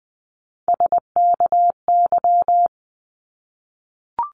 Morse code, heard as a keyed tone, about 20 wpm, 700 Hz. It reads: SKY
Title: Top 1001-1250 Words 20wpm